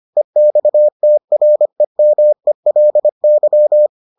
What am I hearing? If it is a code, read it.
Morse code, heard as a keyed tone, about 25 wpm, 600 Hz. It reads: EXTREMELY